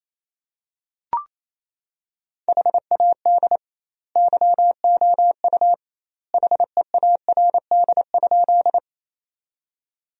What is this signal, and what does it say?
Morse code, heard as a keyed tone, about 28 wpm, 700 Hz. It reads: HAD YOU HEARD?
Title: 2x Word-Spacing - Sentences from Top 400 Words - 28wpm